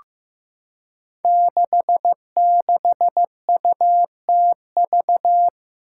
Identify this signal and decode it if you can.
Morse code, heard as a keyed tone, about 15 wpm, 700 Hz. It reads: 66UTV
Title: Five Challenging Letter-Number Combinations 15wpm